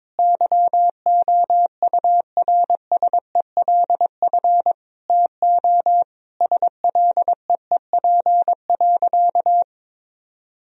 Morse code, heard as a keyed tone, about 22 wpm, 700 Hz. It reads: YOURSELF TO SLEEP.
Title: Sentences from Top 1000 Words - Review Part 2 - 22wpm